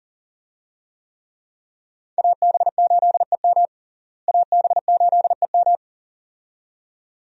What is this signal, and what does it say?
Morse code, heard as a keyed tone, about 40 wpm, 700 Hz. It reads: AB8EM AB8EM
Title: USA Call Signs - 5 Characters - Repeated 2x - 40wpm